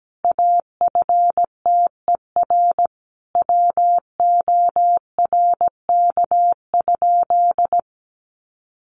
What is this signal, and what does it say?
Morse code, heard as a keyed tone, about 17 wpm, 700 Hz. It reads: AFTER WORK?
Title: Sentences from Top 1400 Words 17wpm